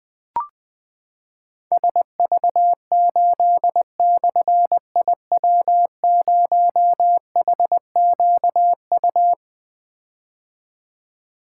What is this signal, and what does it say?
Morse code, heard as a keyed tone, about 20 wpm, 700 Hz. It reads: SV8/IW0HQU
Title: Difficult Call Signs 20wpm